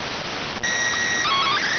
SSTV (slow-scan television), PD 290.